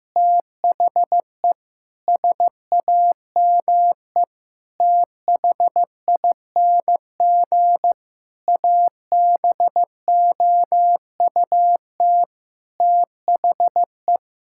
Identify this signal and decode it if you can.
Morse code, heard as a keyed tone, about 16 wpm, 700 Hz. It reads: THE SAME THING ABOUT THE
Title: Sentences from Top 1000 Words - Review Part 5 - 15wpm